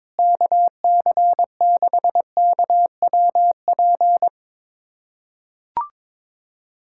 Morse code, heard as a keyed tone, about 22 wpm, 700 Hz. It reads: KC6KWP
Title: USA Callsign DE Callsign - 6 Characters - 22wpm